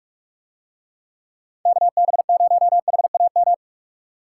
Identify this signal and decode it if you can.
Morse code, heard as a keyed tone, about 48 wpm, 700 Hz. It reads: KB0HAM